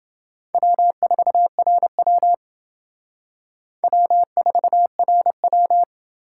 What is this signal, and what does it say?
Morse code, heard as a keyed tone, about 28 wpm, 700 Hz. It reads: W4RW W4RW